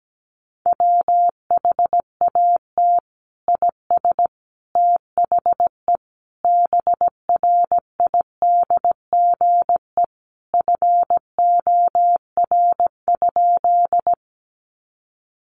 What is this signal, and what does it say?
Morse code, heard as a keyed tone, about 17 wpm, 700 Hz. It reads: WHAT IS THE BRIDGE FOR?
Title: Sentences from Top 1200 Words 17wpm